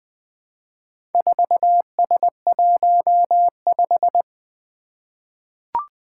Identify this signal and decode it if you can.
Morse code, heard as a keyed tone, about 20 wpm, 700 Hz. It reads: 4S15
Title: Four Challenging Letter-Number Combinations 20wpm